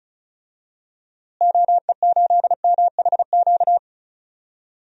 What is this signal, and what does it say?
Morse code, heard as a keyed tone, about 36 wpm, 700 Hz. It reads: OE8MHQ
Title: DXCC-100 Call Signs - 6 Characters - Repeated 2x - 35wpm